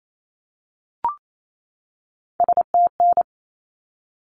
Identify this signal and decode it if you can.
Morse code, heard as a keyed tone, about 28 wpm, 700 Hz. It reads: STN
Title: Abbreviations 28wpm